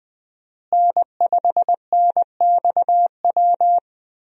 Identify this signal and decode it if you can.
Morse code, heard as a keyed tone, about 20 wpm, 700 Hz. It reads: N5NXW
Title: USA Call Signs - 5 Characters - Repeated 2x - 20wpm